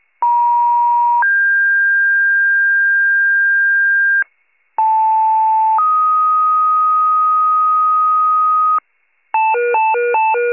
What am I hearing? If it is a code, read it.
Two-tone page, heard as a one-second tone, then a three-second tone, about 950 then 1600 Hz, about 880 then 1250 Hz, with noise. It is heard from 0.2 to 8.8 s.